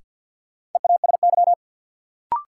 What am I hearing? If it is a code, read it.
Morse code, heard as a keyed tone, about 52 wpm, 700 Hz. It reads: EASY
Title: Sentences from Top 700 Words - New Word Drill - 50wpm